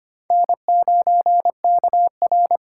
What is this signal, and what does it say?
Morse code, heard as a keyed tone, about 25 wpm, 700 Hz. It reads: N9KR
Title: K1USN SST QSO Practice - 25wpm